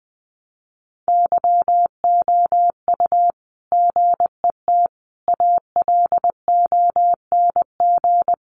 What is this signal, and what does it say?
Morse code, heard as a keyed tone, about 20 wpm, 700 Hz. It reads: YOU GET ALONG
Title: Sentences from Top 700 Words 20wpm